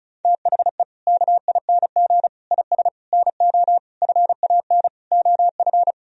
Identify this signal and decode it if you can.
Morse code, heard as a keyed tone, about 36 wpm, 700 Hz. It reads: THE KING IS NO FAN OF
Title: Sentences from Top 1000 Words - Review Part 4 - 35wpm